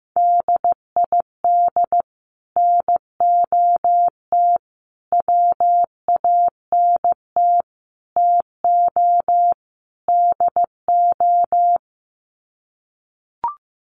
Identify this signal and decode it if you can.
Morse code, heard as a keyed tone, about 15 wpm, 700 Hz. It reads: DID NOT WANT TO DO
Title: Sets of 5 Words Encore 15wpm